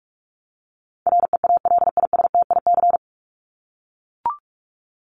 Morse code, heard as a keyed tone, about 44 wpm, 700 Hz. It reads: REALISTIC
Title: Top 1751-2000 Words 45wpm